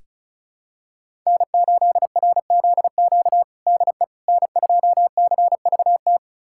Morse code, heard as a keyed tone, about 34 wpm, 700 Hz. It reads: N8RZQ DE N2CVT
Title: USA Callsign DE Callsign - 5 Characters - 35wpm